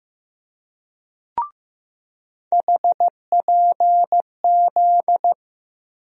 Morse code, heard as a keyed tone, about 16 wpm, 700 Hz. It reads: HPZ
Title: USA Call Sign Postfixes - 3 Letters 15wpm